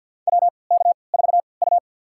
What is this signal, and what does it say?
Morse code, heard as a keyed tone, about 51 wpm, spaced as for 28 wpm, 700 Hz. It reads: WK4U
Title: USA Call Signs - 3x Character Spacing - 50wpm